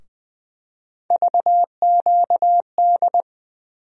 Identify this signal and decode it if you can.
Morse code, heard as a keyed tone, about 20 wpm, 700 Hz. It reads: VQD
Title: USA Call Sign Postfixes - 3 Letters 20wpm